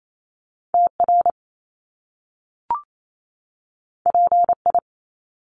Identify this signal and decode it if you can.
Morse code, heard as a keyed tone, about 28 wpm, 700 Hz. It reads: TR PI
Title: Top Two Letter Combinations in Words 28wpm